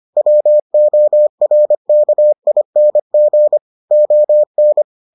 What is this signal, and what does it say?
Morse code, heard as a keyed tone, about 25 wpm, 600 Hz. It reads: WORKING ON